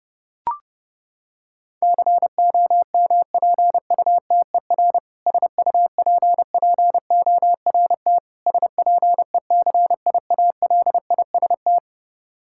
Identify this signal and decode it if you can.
Morse code, heard as a keyed tone, about 30 wpm, 700 Hz. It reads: COMPUTER SUPPORT SPECIALIST
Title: K1USN SST QSO Practice 30wpm